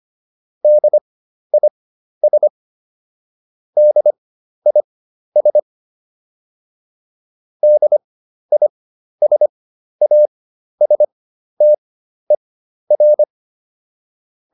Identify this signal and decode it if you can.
Morse code, heard as a keyed tone, about 25 wpm, spaced as for 12 wpm, 600 Hz. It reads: DIS DIS DISASTER